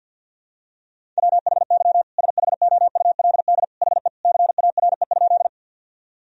Morse code, heard as a keyed tone, about 49 wpm, 700 Hz. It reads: WHY SHOULD HE CARE?
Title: Sentences from Top 1000 Words - Review Part 1 - 50wpm